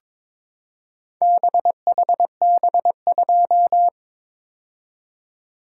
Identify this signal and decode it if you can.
Morse code, heard as a keyed tone, about 22 wpm, 700 Hz. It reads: BHB2
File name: practice / 22wpm / Four Letter-Number Combinations - 22wpm